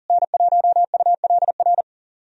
Morse code, heard as a keyed tone, about 40 wpm, 700 Hz. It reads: N1ULR
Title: USA Call Signs - 5 Characters - Repeated 2x - 40wpm